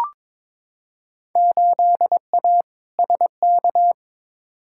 Morse code, heard as a keyed tone, about 22 wpm, 700 Hz. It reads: 8A SK
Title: Field Day Exchanges Encore - 22wpm